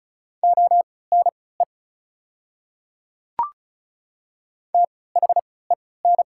Morse code, heard as a keyed tone, about 35 wpm, spaced as for 20 wpm, 690 Hz. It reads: ONE THEN
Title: 3x Character Spacing - Top 100 Words - 35wpm